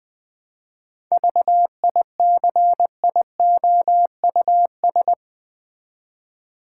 Morse code, heard as a keyed tone, about 20 wpm, 700 Hz. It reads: VICIOUS
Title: 7 Letter Words - First 3 Letters Given - 20wpm